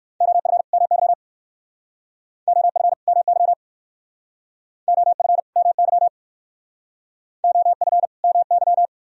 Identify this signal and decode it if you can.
Morse code, heard as a keyed tone, about 54 wpm, 700 Hz. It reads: OF MY OF MY OF MY OF MY
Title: Speed-Racing - Sets of 2 Words 45wpm